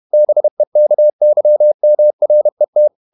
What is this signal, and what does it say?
Morse code, heard as a keyed tone, about 31 wpm, 600 Hz. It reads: BEKYMRET
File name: Ferie QRQ 31wpm